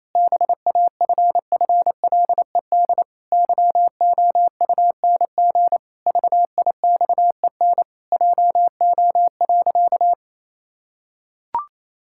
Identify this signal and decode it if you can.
Morse code, heard as a keyed tone, about 28 wpm, 700 Hz. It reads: BAFFLED YOUNG VIXEN JO.
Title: Pangram Challenge v2 - 28wpm